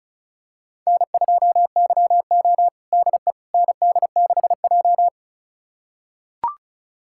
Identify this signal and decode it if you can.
Morse code, heard as a keyed tone, about 34 wpm, 700 Hz. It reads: N2YO DE ND6J
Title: USA Callsign DE Callsign - 4 Characters - 35wpm